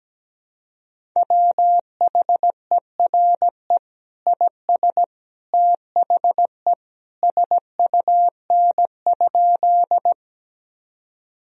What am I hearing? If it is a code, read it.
Morse code, heard as a keyed tone, about 17 wpm, 700 Hz. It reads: WHERE IS THE SUN?
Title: Sentences from Top 1000 Words - Review Part 1 - 17wpm